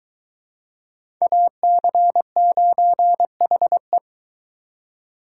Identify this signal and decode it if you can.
Morse code, heard as a keyed tone, about 21 wpm, 700 Hz. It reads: AC9HE